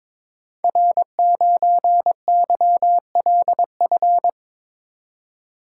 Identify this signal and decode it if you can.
Morse code, heard as a keyed tone, about 22 wpm, 700 Hz. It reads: R9YLF
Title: Code Groups - 5 Letter:Numbers - 22wpm